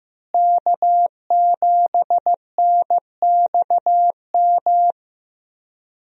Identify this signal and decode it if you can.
Morse code, heard as a keyed tone, about 15 wpm, 700 Hz. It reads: K7NXM
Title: USA Call Signs - 5 Characters - Repeated 2x - 15wpm